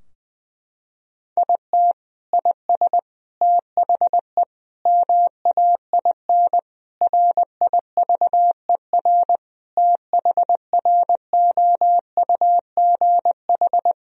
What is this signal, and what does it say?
Morse code, heard as a keyed tone, about 20 wpm, 700 Hz. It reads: IT IS THE MAIN RIVER THROUGH
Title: Sentences from Top 1000 Words - Review Part 2 - 20wpm